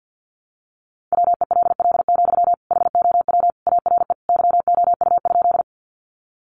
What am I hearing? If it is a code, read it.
Morse code, heard as a keyed tone, about 50 wpm, 700 Hz. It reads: WELL, HOW ARE YOU?
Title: Sentences from Top 200 Words 50wpm